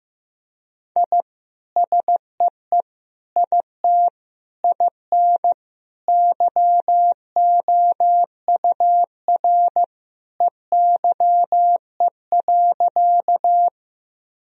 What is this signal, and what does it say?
Morse code, heard as a keyed tone, about 15 wpm, 700 Hz. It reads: I SEE IT IN YOUR EYE.
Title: Sentences from Top 1000 Words - Review Part 1 - 15wpm